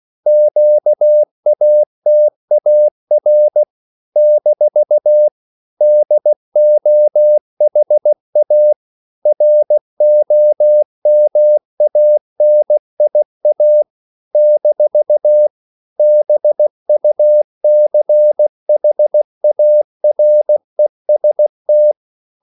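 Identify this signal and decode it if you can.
Morse code, heard as a keyed tone, about 16 wpm, 600 Hz. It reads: QATAR - DOHA ROMANIA - BUCHAREST
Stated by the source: CW practice of national capitals at 16 words per minute, from ON6ZQ.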